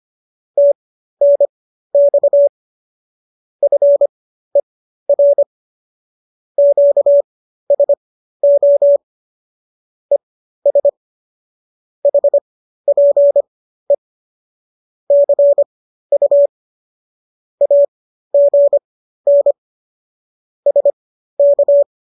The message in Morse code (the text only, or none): TNX FER QSO ES HPE CU AGN SK